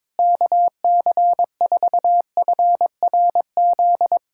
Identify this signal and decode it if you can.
Morse code, heard as a keyed tone, about 22 wpm, 700 Hz. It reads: KC4FRZ
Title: USA Callsign DE Callsign - 6 Characters - 22wpm